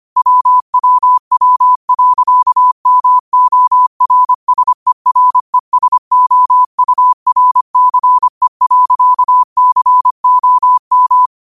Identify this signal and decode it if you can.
Morse code, heard as a keyed tone, about 25 wpm, 1000 Hz. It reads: WWW.MORSERESOURCE.COM
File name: qotdfiles / 25wpm